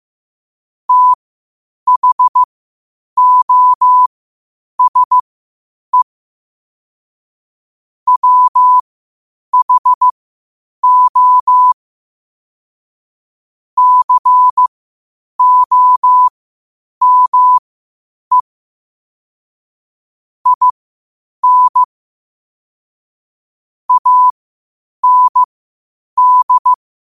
Morse code, heard as a keyed tone, about 15 wpm, spaced as for 8 wpm, 1000 Hz. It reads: THOSE WHO COME IN AND
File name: qotdfiles / 8wpm